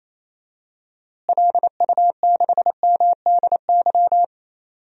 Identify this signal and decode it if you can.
Morse code, heard as a keyed tone, about 28 wpm, 700 Hz. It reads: LU6MDY
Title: DXCC-100 Call Signs - 6 Characters - Repeated 3x - 28wpm